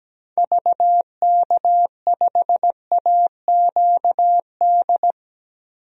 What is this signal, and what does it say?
Morse code, heard as a keyed tone, about 17 wpm, 700 Hz. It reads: VK5AQD
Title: DXCC-100 Call Signs - 6 Characters - 17wpm